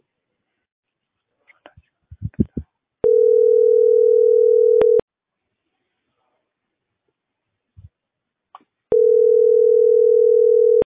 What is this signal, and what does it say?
Ringback tone (North America), 440 + 480 Hz.